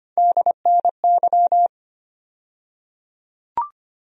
Morse code, heard as a keyed tone, about 25 wpm, 700 Hz. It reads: DNY
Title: USA Call Sign Postfixes - 3 Letters 25wpm